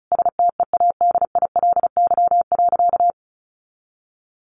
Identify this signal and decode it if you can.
Morse code, heard as a keyed tone, about 34 wpm, 700 Hz. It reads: STEADILY.
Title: A Sound Of Thunder 35wpm